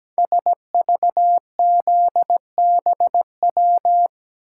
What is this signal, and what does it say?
Morse code, heard as a keyed tone, about 17 wpm, 700 Hz. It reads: SVZBW
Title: Code Groups - 5 Letters - 17wpm